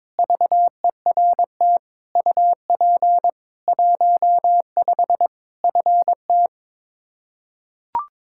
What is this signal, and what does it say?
Morse code, heard as a keyed tone, about 22 wpm, 700 Hz. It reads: VERT UP 15 FT
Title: Practice QSO - Exchange 2 - Info - 22wpm